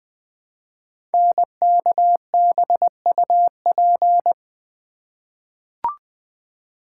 Morse code, heard as a keyed tone, about 20 wpm, 700 Hz. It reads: NKBUP
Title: Code Groups - 5 Letters - 20wpm